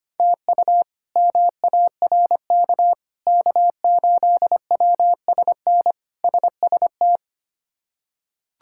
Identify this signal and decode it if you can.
Morse code, heard as a keyed tone, about 25 wpm, 700 Hz. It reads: TU MARK K8WSN SST
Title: K1USN SST QSO Practice - 25wpm